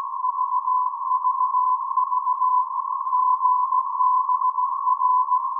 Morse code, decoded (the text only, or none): VVV GBR